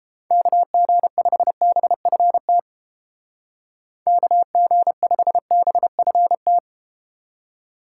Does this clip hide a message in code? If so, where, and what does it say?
Yes: Morse code from 0.3 to 6.6 s, heard as a keyed tone, about 30 wpm, 700 Hz. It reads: KG5BFT KG5BFT